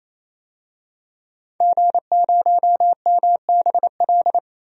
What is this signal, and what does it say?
Morse code, heard as a keyed tone, about 28 wpm, 700 Hz. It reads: G0MBL
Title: DXCC-100 Call Signs - 5 Characters - 28wpm